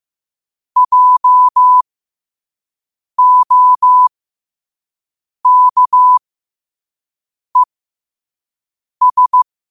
Morse code, heard as a keyed tone, about 15 wpm, spaced as for 5 wpm, 1000 Hz. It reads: JOKES
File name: qotdfiles / 5wpm